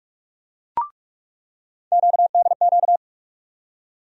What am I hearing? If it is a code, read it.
Morse code, heard as a keyed tone, about 47 wpm, 700 Hz. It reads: QDQ
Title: USA Call Sign Postfixes - 3 Letters 45wpm